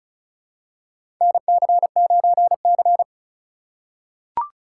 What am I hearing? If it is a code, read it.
Morse code, heard as a keyed tone, about 36 wpm, 700 Hz. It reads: NC9C
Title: USA Call Signs - 4 Characters - 4x Character Spacing - 35wpm